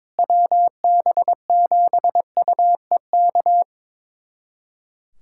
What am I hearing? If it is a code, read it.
Morse code, heard as a keyed tone, about 22 wpm, 700 Hz. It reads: WB7UEK
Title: USA Callsign DE Callsign 22wpm